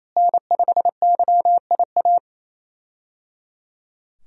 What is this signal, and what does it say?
Morse code, heard as a keyed tone, about 28 wpm, 700 Hz. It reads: N5YIA
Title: USA Call Signs - 5 Characters - Repeated 3x - 28wpm